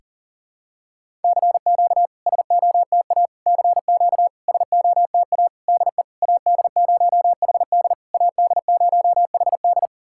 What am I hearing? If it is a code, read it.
Morse code, heard as a keyed tone, about 40 wpm, 690 Hz. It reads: CQ SOTA CQ SOTA DE AD0HD AD0HD
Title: SOTA QSO Practice 40wpm